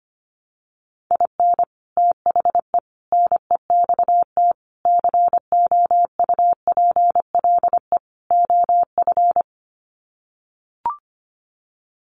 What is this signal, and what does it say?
Morse code, heard as a keyed tone, about 25 wpm, 700 Hz. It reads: IN THE NEXT COUPLE OF
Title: Sets of 5 Words Encore2 25wpm